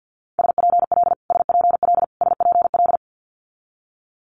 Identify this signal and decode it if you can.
Morse code, heard as a keyed tone, about 50 wpm, 700 Hz. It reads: SPL SPL SPL
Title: Multi-letter Phonemes Repeated 3x 50wpm